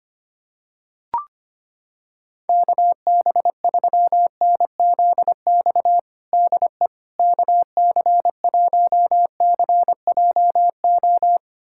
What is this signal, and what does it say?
Morse code, heard as a keyed tone, about 25 wpm, 700 Hz. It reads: KB3NZX DE KC1CJO
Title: USA Callsign DE Callsign 25wpm